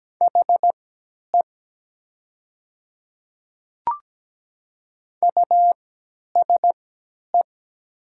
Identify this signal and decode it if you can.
Morse code, heard as a keyed tone, about 17 wpm, spaced as for 10 wpm, 700 Hz. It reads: HE USE
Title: 3x Character Spacing - Top 100 Words - 17wpm